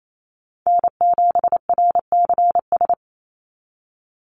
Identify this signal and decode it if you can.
Morse code, heard as a keyed tone, about 28 wpm, 700 Hz. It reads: N7RCS
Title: Call Signs Repeated 3x - 28wpm